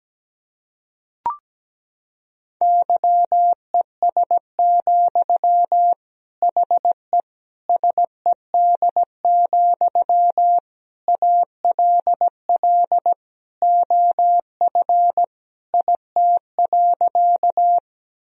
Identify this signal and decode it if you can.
Morse code, heard as a keyed tone, about 17 wpm, 700 Hz. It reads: YES, HE SED, ALL OF IT.
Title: Sentences from Top 500 Words and Abbreviations 17wpm